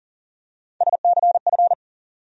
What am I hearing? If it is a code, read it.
Morse code, heard as a keyed tone, about 40 wpm, 700 Hz. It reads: SCF
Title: USA Call Sign Postfixes - 3 Letters 40wpm